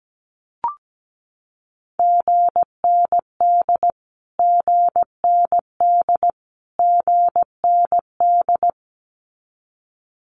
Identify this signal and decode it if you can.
Morse code, heard as a keyed tone, about 17 wpm, 700 Hz. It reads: GND GND GND
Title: Abbreviations Repeated 3x - 17wpm